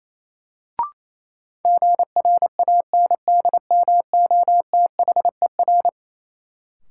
Morse code, heard as a keyed tone, about 28 wpm, 700 Hz. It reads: GRANDMOTHER
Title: Sentences from Top 1200 Words - New Word Drill Alt - 28wpm